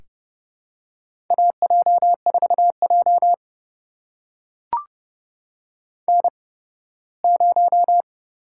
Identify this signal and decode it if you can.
Morse code, heard as a keyed tone, about 30 wpm, spaced as for 18 wpm, 700 Hz. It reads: AJ4J N0
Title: USA Call Signs - 4 Characters - 8x Character Spacing - 30wpm